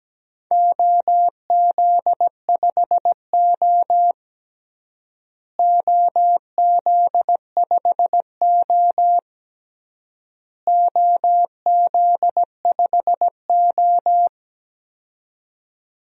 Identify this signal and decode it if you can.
Morse code, heard as a keyed tone, about 17 wpm, 700 Hz. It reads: OZ5O OZ5O OZ5O
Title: DXCC-100 Call Signs - 4 Characters - Repeated 3x - 17wpm